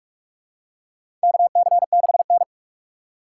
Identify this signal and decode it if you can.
Morse code, heard as a keyed tone, about 42 wpm, 700 Hz. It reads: KC6N
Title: USA Call Signs - 4 Characters - 6x Character Spacing - 45wpm